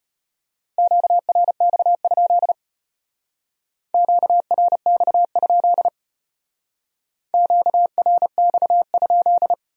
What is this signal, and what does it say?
Morse code, heard as a keyed tone, about 33 wpm, 700 Hz. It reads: QRX? QRX? QRX?